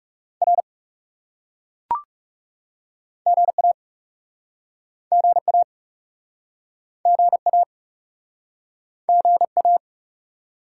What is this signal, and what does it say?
Morse code, heard as a keyed tone, about 42 wpm, 700 Hz. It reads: R GA GA GA GA